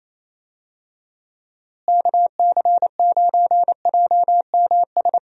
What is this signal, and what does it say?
Morse code, heard as a keyed tone, about 27 wpm, 700 Hz. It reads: KC9JMS
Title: USA Call Signs - 6 Characters - Repeated 2x - 28wpm